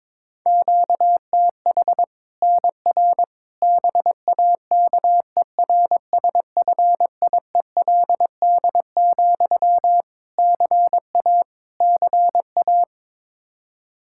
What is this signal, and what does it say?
Morse code, heard as a keyed tone, about 22 wpm, 700 Hz. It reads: QTH NR BAKERSFIELD, CA CA
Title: Practice QSO - Exchange 1 - Beginning - 22wpm